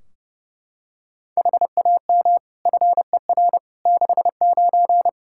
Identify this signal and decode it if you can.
Morse code, heard as a keyed tone, about 32 wpm, 700 Hz. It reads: HAM FER 69
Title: Practice QSO - Exchange 3 - More Info - 30wpm